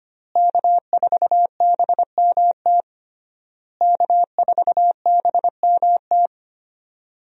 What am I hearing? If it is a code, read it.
Morse code, heard as a keyed tone, about 25 wpm, 700 Hz. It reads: K4BMT K4BMT
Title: USA Call Signs - 5 Characters - Repeated 3x - 25wpm